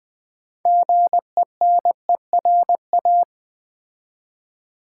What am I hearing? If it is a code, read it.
Morse code, heard as a keyed tone, about 20 wpm, 700 Hz. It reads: GENERA
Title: Build A Word - 2 Letters At A Time - 8-Letter Words - 20wpm